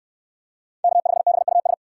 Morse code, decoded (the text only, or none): K5BUS